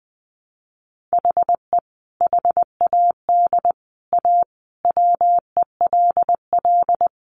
Morse code, heard as a keyed tone, about 20 wpm, 700 Hz. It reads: HE HAD A WELL
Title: Sentences from Top 1300 Words 20wpm